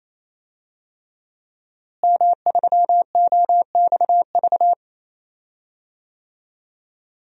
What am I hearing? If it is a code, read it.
Morse code, heard as a keyed tone, about 27 wpm, 700 Hz. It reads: M3OXV
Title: DXCC-100 Call Signs - 5 Characters - Repeated 3x - 28wpm